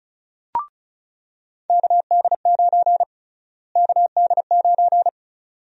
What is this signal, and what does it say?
Morse code, heard as a keyed tone, about 36 wpm, 700 Hz. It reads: KD9 KD9
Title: USA Call Sign Prefixes - Repeated 2x - 35wpm